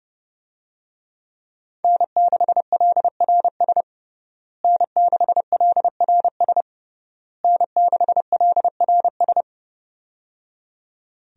Morse code, heard as a keyed tone, about 30 wpm, 700 Hz. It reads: N6LRS N6LRS N6LRS
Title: USA Call Signs - 5 Characters - Repeated 3x - 30wpm